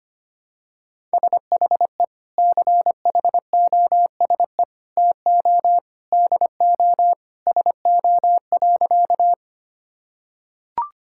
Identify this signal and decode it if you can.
Morse code, heard as a keyed tone, about 25 wpm, 700 Hz. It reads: SHE CHOSE TO DO SO.
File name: practice / 25wpm / Sentences with 3-Word Phrases - 25wpm